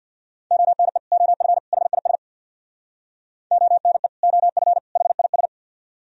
Morse code, heard as a keyed tone, about 67 wpm, 700 Hz. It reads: ONE OF HIS ONE OF HIS